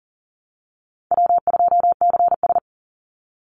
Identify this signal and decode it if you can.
Morse code, heard as a keyed tone, about 40 wpm, 700 Hz. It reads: W2CS
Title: Easy Call Signs - 40wpm